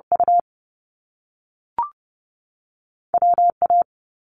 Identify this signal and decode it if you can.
Morse code, heard as a keyed tone, about 30 wpm, 700 Hz. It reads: U W A
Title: Top Two Letter Combinations in Words 30wpm